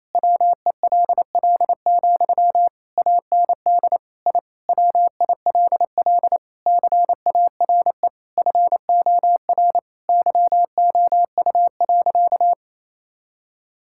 Morse code, heard as a keyed tone, about 28 wpm, 700 Hz. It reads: WELL, AND I WILL CARE FOR YOU.
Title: Sentences from Top 1000 Words - Review Part 6 - 28wpm